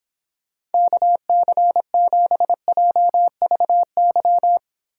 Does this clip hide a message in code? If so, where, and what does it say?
Yes: Morse code from 0.7 to 4.6 s, heard as a keyed tone, about 29 wpm, 700 Hz. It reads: KC7JVY